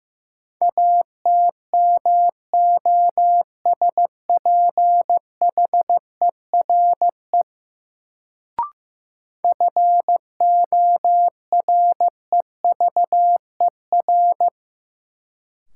Morse code, heard as a keyed tone, about 15 wpm, 700 Hz. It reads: ATMOSPHERE FOREVER
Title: Sentences from Top 1600 Words - New Word Drill Alt - 15wpm